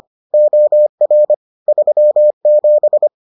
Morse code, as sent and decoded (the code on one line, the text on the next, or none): --- .-. / ...-- --...
OR 37